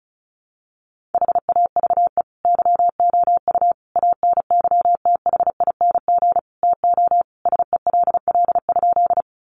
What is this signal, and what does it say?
Morse code, heard as a keyed tone, about 34 wpm, 700 Hz. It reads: HAVE YOU ANYTHING TO SELL?
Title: Sentences from Top 1100 Words 35wpm